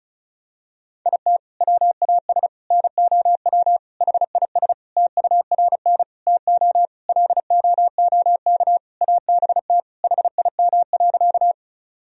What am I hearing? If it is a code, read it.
Morse code, heard as a keyed tone, about 34 wpm, 700 Hz. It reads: IT WAS NOW HIS TURN TO LOOK ABT HIM.
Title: Sentences from Top 200 Words and Abbreviations 35wpm